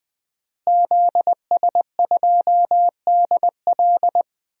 Morse code, heard as a keyed tone, about 20 wpm, 700 Hz. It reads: ZS2DL
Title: DXCC-100 Call Signs - 5 Characters - Repeated 2x - 20wpm